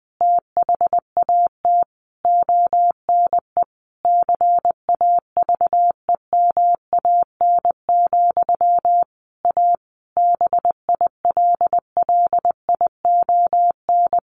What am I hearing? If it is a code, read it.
Morse code, heard as a keyed tone, about 20 wpm, 700 Hz. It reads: THAT ONE CAVEMAN, A BILLION
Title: A Sound Of Thunder 20wpm